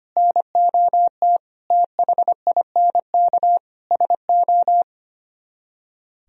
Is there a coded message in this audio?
Yes: Morse code from 0.2 to 4.8 s, heard as a keyed tone, about 33 wpm, 700 Hz. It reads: NOT THINK SO